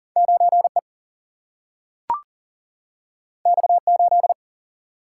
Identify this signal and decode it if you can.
Morse code, heard as a keyed tone, about 40 wpm, 700 Hz. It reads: 9E X8
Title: Single-Letter Single-Number 40wpm